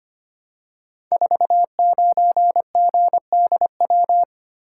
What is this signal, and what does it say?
Morse code, heard as a keyed tone, about 25 wpm, 700 Hz. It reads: 49GDW
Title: Code Groups - 5 Letter:Numbers - 25wpm